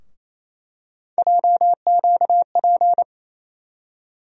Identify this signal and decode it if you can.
Morse code, heard as a keyed tone, about 28 wpm, 700 Hz. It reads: JQP
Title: USA Call Sign Postfixes - 3 Letters 28wpm